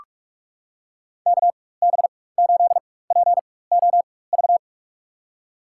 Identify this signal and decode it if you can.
Morse code, heard as a keyed tone, about 46 wpm, spaced as for 21 wpm, 700 Hz. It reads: KB8POV
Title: USA Call Signs - 6 Characters - 4x Character Spacing - 45wpm